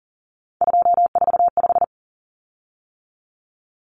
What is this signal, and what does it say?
Morse code, heard as a keyed tone, about 40 wpm, 700 Hz. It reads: 245
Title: Common Cut Numbers - 3-Digit 40wpm